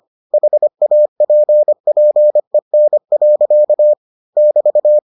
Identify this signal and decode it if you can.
Morse code, heard as a keyed tone, about 25 wpm, 600 Hz. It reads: HAPPEN. <BT>